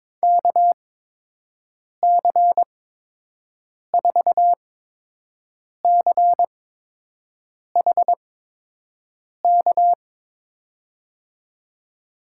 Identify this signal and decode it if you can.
Morse code, heard as a keyed tone, about 22 wpm, spaced as for 6 wpm, 700 Hz. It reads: KC4CHK
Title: USA Call Signs - 6 Characters - 8x Character Spacing - 22wpm